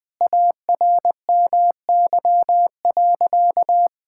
Morse code, heard as a keyed tone, about 21 wpm, 700 Hz. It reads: ARMY.
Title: Sentences from Top 1000 Words - Review Part 6 - 20wpm